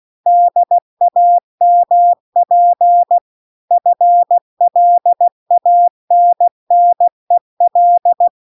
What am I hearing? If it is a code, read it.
Morse code, heard as a keyed tone, about 16 wpm, 700 Hz. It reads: DAMP FLANNEL
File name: CW-The-Big-Twoheaded-River-2-16WPM